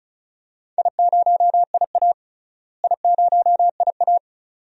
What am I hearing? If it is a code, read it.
Morse code, heard as a keyed tone, about 34 wpm, 700 Hz. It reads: I0IA I0IA
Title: DXCC-100 Call Signs - 4 Characters - Repeated 2x - 35wpm